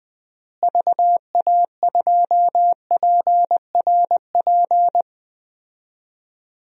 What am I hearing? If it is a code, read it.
Morse code, heard as a keyed tone, about 20 wpm, 700 Hz. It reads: VA2PRP
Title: DXCC-100 Call Signs - 6 Characters - Repeated 2x - 20wpm